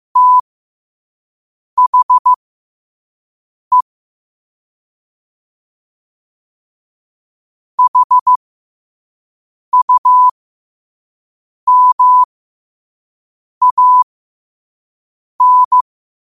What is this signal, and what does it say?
Morse code, heard as a keyed tone, about 15 wpm, spaced as for 5 wpm, 1000 Hz. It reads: THE HUMAN
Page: New quotes every day in morse code at 5 Words per minute.